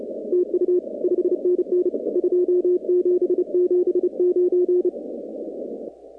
Morse code, heard as a keyed tone, about 32 wpm, 360 Hz. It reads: TU 5NN 2779
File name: contests / 2019-cw-wpx-kt5j-k5pi- -2330 0-41